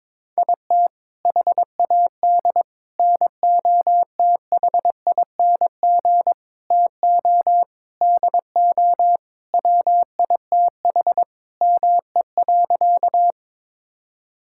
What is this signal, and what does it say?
Morse code, heard as a keyed tone, about 22 wpm, 700 Hz. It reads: IT HAD NOTHING TO DO WITH ME.
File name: practice / 22wpm / Sentences with 4-Word Phrases - 22wpm